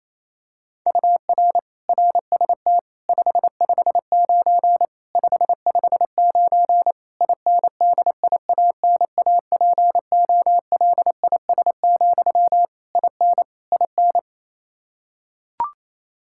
Morse code, heard as a keyed tone, about 27 wpm, 700 Hz. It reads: UR RST 559 559 INDIANAPOLIS, IN IN
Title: Practice QSO - Exchange 1 - Beginning - 28wpm